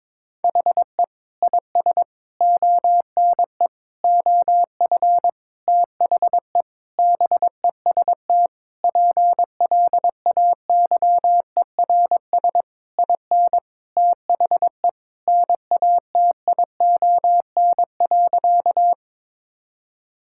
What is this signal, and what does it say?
Morse code, heard as a keyed tone, about 22 wpm, 700 Hz. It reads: HE IS ONE OF THE BEST PLAYERS IN THE NATION.
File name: Sentences with 7-Word Phrases - 22wpm